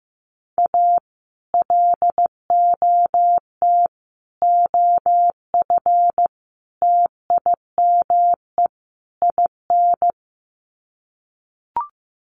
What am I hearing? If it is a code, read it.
Morse code, heard as a keyed tone, about 15 wpm, 700 Hz. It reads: A LOT OF TIME IN
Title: Sets of 5 Words Encore2 15wpm